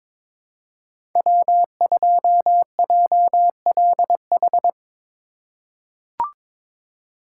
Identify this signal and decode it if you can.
Morse code, heard as a keyed tone, about 22 wpm, 700 Hz. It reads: W2JLH
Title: USA Call Signs - 5 Characters - 2x Character Spacing - 22wpm